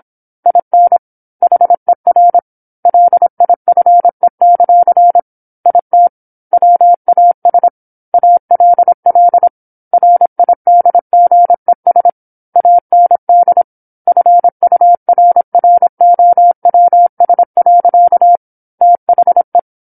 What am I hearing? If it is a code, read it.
Morse code, heard as a keyed tone, about 27 wpm, 700 Hz. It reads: IN HER LIFE; IT WAS ALL RIDGES AND FURROWS. THE
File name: 08-Chapter-08 26 WPM